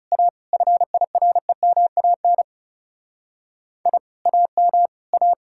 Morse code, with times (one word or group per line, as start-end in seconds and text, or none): 0.1-0.3 A
0.5-2.4 FIREMAN
3.9-4.0 I
4.3-4.9 AM
5.1-5.3 A